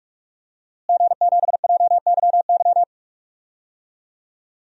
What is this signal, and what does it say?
Morse code, heard as a keyed tone, about 46 wpm, 700 Hz. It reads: G7JYY
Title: DXCC-100 Call Signs - 5 Characters - Repeated 2x - 45wpm